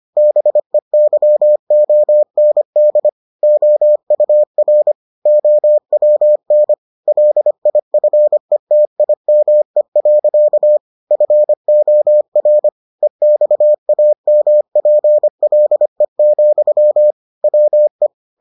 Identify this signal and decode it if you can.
Morse code, heard as a keyed tone, about 25 wpm, 600 Hz. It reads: BEYOND OUR OWN LIFETIME. FOR EXAMPLE, WE